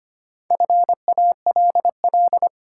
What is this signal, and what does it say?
Morse code, heard as a keyed tone, about 25 wpm, 700 Hz. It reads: FALL
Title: Sentences from Top 500 Words - New Word Drill - 25wpm